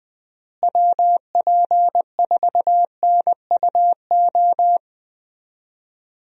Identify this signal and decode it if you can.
Morse code, heard as a keyed tone, about 20 wpm, 700 Hz. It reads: WP4NUO
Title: USA Call Signs - 6 Characters - Repeated 3x - 20wpm